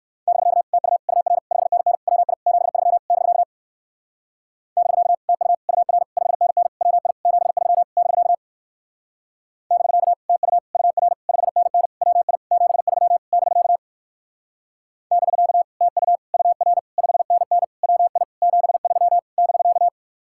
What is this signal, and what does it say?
Morse code, heard as a keyed tone, about 59 wpm, 700 Hz. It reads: <BK> TU UR 5NN WI 73 <BK> <BK> TU UR 5NN WI 73 <BK> <BK> TU UR 5NN WI 73 <BK> <BK> TU UR 5NN WI 73 <BK>